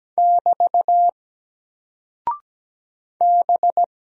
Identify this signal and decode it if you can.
Morse code, heard as a keyed tone, about 17 wpm, 700 Hz. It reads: <BT> B
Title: Single Character (Letter:Number:Punctuation:Prosign) Alt 17wpm